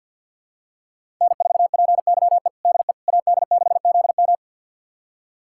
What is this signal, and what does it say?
Morse code, heard as a keyed tone, about 50 wpm, 700 Hz. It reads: N4PYE DE AD6ZM
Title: USA Callsign DE Callsign - 5 Characters - 50wpm